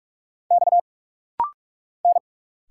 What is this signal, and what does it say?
Morse code, heard as a keyed tone, about 46 wpm, 700 Hz. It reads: X N
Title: Single Letters - Rapid-Fire 45wpm